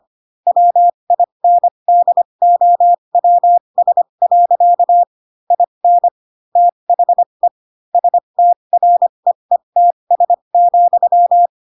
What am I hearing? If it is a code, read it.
Morse code, heard as a keyed tone, about 25 wpm, spaced as for 22 wpm, 700 Hz. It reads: WINDOWS. IN THE STREETS,
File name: Pinocchio / Pinocchio ch6 18 25 ik1hge